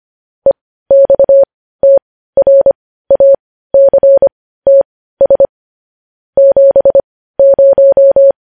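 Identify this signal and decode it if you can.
Morse code, heard as a keyed tone, about 25 wpm, spaced as for 15 wpm, 550 Hz. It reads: EXTRACTS 70
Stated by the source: Morse news headlines practice at 15.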